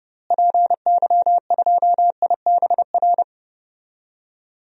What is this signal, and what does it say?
Morse code, heard as a keyed tone, about 30 wpm, 700 Hz. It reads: PY2IBR
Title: DXCC-100 Call Signs - 6 Characters - Repeated 2x - 30wpm